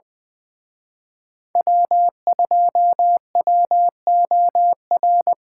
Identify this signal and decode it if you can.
Morse code, heard as a keyed tone, about 23 wpm, 700 Hz. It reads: W2WOR
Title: Speed Racing - USA 1x3 Call Signs 20wpm